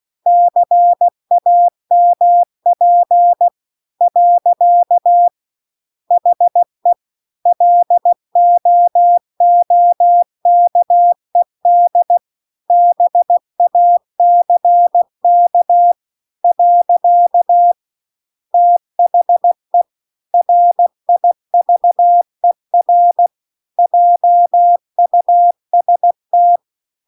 Morse code, heard as a keyed tone, about 16 wpm, 700 Hz. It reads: CAMP . HE LOOKED BACK . THE RIVER JUST
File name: CW-The-Big-Twoheaded-River-2-16WPM